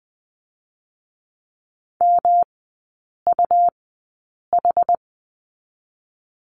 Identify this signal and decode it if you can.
Morse code, heard as a keyed tone, about 20 wpm, 700 Hz. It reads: M U H
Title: 184 Lesson 30e - Letter b - Three Characters 20wpm